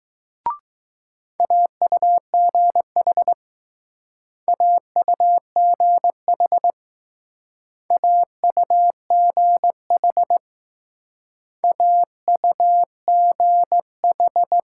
Morse code, heard as a keyed tone, about 20 wpm, 700 Hz. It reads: AUGH AUGH AUGH AUGH